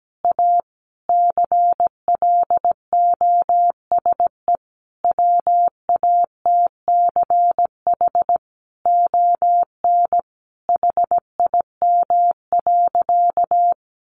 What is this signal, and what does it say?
Morse code, heard as a keyed tone, about 17 wpm, 700 Hz. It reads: A CLOSE WATCH ON HIM.
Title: Sentences from Top 1300 Words 17wpm